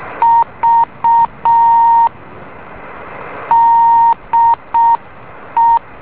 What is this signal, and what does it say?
Morse code, heard as a keyed tone, about 6 wpm, 920 Hz. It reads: V DE
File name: beacon / JH8ZND